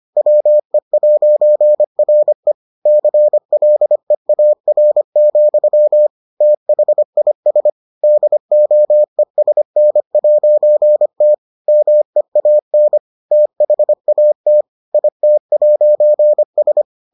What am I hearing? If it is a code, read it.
Morse code, heard as a keyed tone, about 25 wpm, 600 Hz. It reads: WE'RE CLEAR, THIS DOESN'T MEAN THAT IT'S